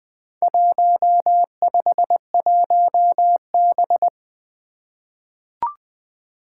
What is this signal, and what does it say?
Morse code, heard as a keyed tone, about 20 wpm, 700 Hz. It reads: 151B
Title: Four Challenging Letter-Number Combinations 20wpm